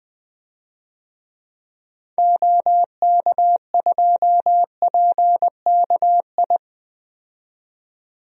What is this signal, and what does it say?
Morse code, heard as a keyed tone, about 20 wpm, 700 Hz. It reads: OK2PKI
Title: DXCC-100 Call Signs - 6 Characters - 20wpm